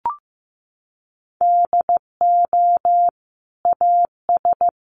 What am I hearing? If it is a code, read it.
Morse code, heard as a keyed tone, about 15 wpm, 700 Hz. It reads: DO AS
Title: Sentences from Top 1500 Words 15wpm